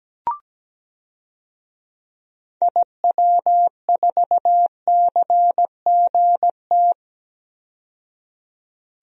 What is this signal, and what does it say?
Morse code, heard as a keyed tone, about 17 wpm, 700 Hz. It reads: IW4CGT
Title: DXCC-100 Call Signs - 6 Characters - 17wpm